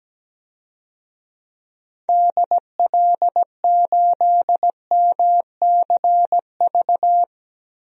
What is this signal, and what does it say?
Morse code, heard as a keyed tone, about 17 wpm, 700 Hz. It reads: DL8MCV
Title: DXCC-100 Call Signs - 6 Characters - 17wpm